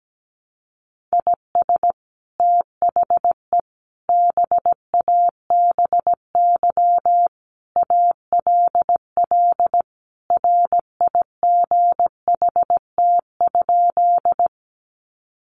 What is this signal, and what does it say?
Morse code, heard as a keyed tone, about 17 wpm, 700 Hz. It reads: IS THE BABY ALL RIGHT?
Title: Sentences from Top 1300 Words 17wpm